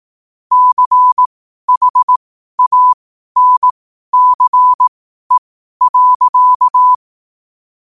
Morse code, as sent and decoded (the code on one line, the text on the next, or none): -.-. .... .- -. -.-. .-.-.-
CHANCE.